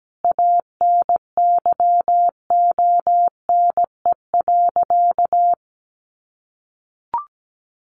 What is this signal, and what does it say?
Morse code, heard as a keyed tone, about 17 wpm, 700 Hz. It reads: ANYONE.
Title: Sentences from Top 1500 Words 17wpm